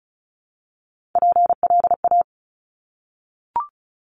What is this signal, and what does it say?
Morse code, heard as a keyed tone, about 36 wpm, 700 Hz. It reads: PLA
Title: Top Three Letter Combinations in Words 35wpm